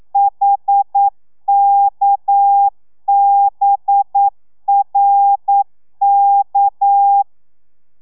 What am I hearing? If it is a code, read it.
Morse code, heard as a keyed tone, about 9 wpm, 800 Hz. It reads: HKBRK
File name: telegrafia / i2tfj / Lez6 40